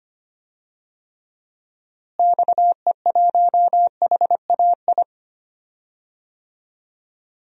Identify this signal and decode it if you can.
Morse code, heard as a keyed tone, about 25 wpm, 700 Hz. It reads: XE1HAI
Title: DXCC-100 Call Signs - 6 Characters - Repeated 3x - 25wpm